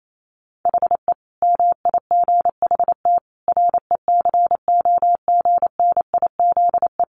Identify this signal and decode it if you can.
Morse code, heard as a keyed tone, about 28 wpm, 700 Hz. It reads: HE MIGHT RECOGNIZE
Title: Sentences from Top 1400 Words 28wpm